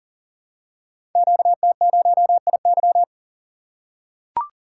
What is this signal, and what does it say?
Morse code, heard as a keyed tone, about 42 wpm, 690 Hz. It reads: QT0IY
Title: Code Groups - 5 Letter:Numbers - 40wpm